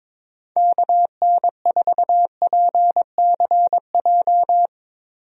Morse code, heard as a keyed tone, about 22 wpm, 700 Hz. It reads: KN4PCJ
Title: USA Call Signs - 6 Characters - Repeated 3x - 22wpm